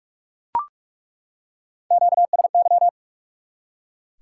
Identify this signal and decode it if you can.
Morse code, heard as a keyed tone, about 47 wpm, 700 Hz. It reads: QSY
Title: Top 180 QSO Elements 45wpm